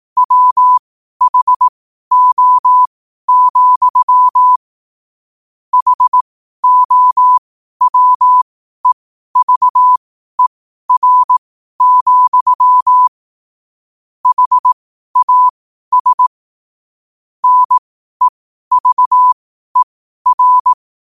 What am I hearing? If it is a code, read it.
Morse code, heard as a keyed tone, about 18 wpm, spaced as for 12 wpm, 1000 Hz. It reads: WHO, HOWEVER, HAS NEVER
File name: qotdfiles / 12wpm